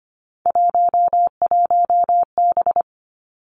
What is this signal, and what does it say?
Morse code, heard as a keyed tone, about 25 wpm, 700 Hz. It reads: 11B
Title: Three Challenging Letter-Number Combinations 25wpm